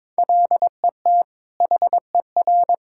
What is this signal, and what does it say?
Morse code, heard as a keyed tone, about 22 wpm, 700 Hz. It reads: LET HER
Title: Sentences from Top 1000 Words - Review Part 6 - 22wpm